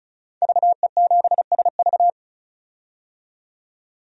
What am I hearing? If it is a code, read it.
Morse code, heard as a keyed tone, about 36 wpm, 700 Hz. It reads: VE7SV
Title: DXCC-100 Call Signs - 5 Characters - 35wpm